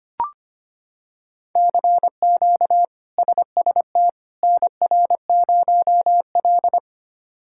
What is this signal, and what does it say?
Morse code, heard as a keyed tone, about 25 wpm, 700 Hz. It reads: CQ SST NR0L
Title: K1USN SST QSO Practice - 25wpm